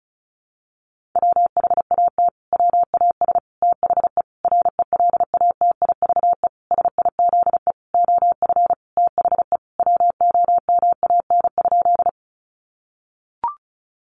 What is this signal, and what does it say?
Morse code, heard as a keyed tone, about 34 wpm, 700 Hz. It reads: WHAT WAS THE RELATIVE SIZE OF THE WOMAN?
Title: Sentences from Top 1400 Words 35wpm